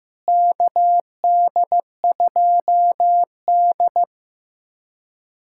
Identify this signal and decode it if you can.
Morse code, heard as a keyed tone, about 15 wpm, 700 Hz. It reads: KD2D
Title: USA Callsign DE Callsign - 4 Characters - 15wpm